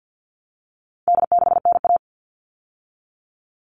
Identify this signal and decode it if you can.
Morse code, heard as a keyed tone, about 50 wpm, 700 Hz. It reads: D6NA
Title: All Cut Numbers - 4-Digit 50wpm